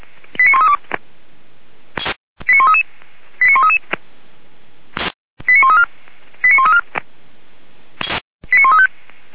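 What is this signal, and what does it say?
Five-tone selective call (ZVEI), heard as quick runs of five tones, tones of about 80 ms. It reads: 89132 89133 89133 89134 89134 89135
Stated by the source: ZVEI five-tone selective call received with an SDR.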